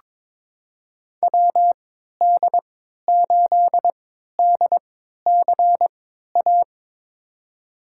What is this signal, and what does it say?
Morse code, heard as a keyed tone, about 22 wpm, spaced as for 13 wpm, 700 Hz. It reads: WD8DCA
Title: USA Call Signs - 3x Character Spacing - 22wpm